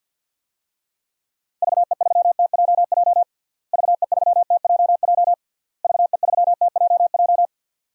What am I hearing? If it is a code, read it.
Morse code, heard as a keyed tone, about 46 wpm, 700 Hz. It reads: VE3TJJ VE3TJJ VE3TJJ
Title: DXCC-100 Call Signs - 6 Characters - Repeated 3x - 50wpm